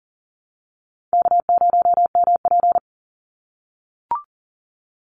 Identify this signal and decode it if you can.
Morse code, heard as a keyed tone, about 40 wpm, 700 Hz. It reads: K0MP
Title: NA SOTA Chaser Callsigns 40wpm